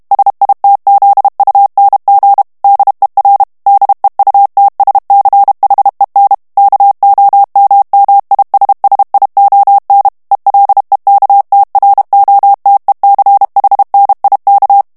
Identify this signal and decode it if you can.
Morse code, heard as a keyed tone, about 32 wpm, 800 Hz. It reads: SITZUNG DER DEUTSCHEN KOMMISSION ELEKTROTECHNIK
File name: ALLEMAND texte clair 1 vitesse 30 mots minute